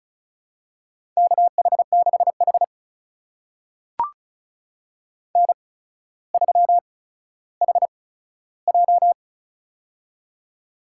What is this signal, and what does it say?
Morse code, heard as a keyed tone, about 36 wpm, spaced as for 15 wpm, 700 Hz. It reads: KH6H N3HJ
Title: USA Call Signs - 4 Characters - 8x Character Spacing - 35wpm